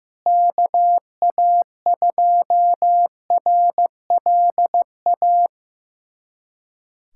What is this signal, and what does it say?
Morse code, heard as a keyed tone, about 15 wpm, 700 Hz. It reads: KA2RLA
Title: USA Callsign DE Callsign 15wpm